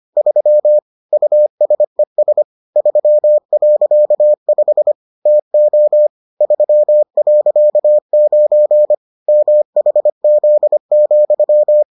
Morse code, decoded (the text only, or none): 3 USES 3.5 TO 3.9 MHZ,